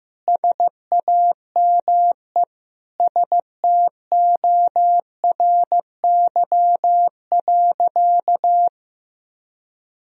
Morse code, heard as a keyed tone, about 16 wpm, 700 Hz. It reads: SAME STORY.
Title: Sentences from Top 1000 Words - Review Part 1 - 15wpm